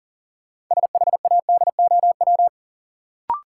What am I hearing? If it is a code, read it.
Morse code, heard as a keyed tone, about 40 wpm, 700 Hz. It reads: SHADOW
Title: Sets of 5 Words Encore - New Word Drill - 40wpm